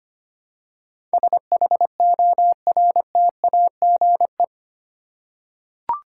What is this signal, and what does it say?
Morse code, heard as a keyed tone, about 25 wpm, 700 Hz. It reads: SHORTAGE
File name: practice / 25wpm / 8 Letter Words - First 3 Letters Given - 25wpm